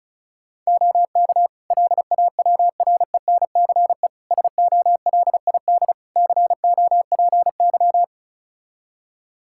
Morse code, heard as a keyed tone, about 36 wpm, 700 Hz. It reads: OK LAWRENCE SOLID COPY
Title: Practice QSO - Exchange 3 - More Info - 35wpm